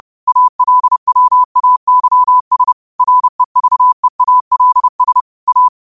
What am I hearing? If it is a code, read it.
Morse code, heard as a keyed tone, about 30 wpm, 1000 Hz. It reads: ALWAYS REVEALS A